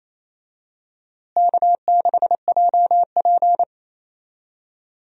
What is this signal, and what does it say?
Morse code, heard as a keyed tone, about 28 wpm, 700 Hz. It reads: K6JP
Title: USA Call Signs - 4 Characters - Repeated 3x - 28wpm